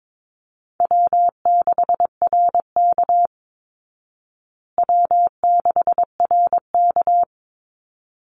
Morse code, heard as a keyed tone, about 22 wpm, 690 Hz. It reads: W6RK W6RK
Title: Easy Call Signs Repeated 2x - 22wpm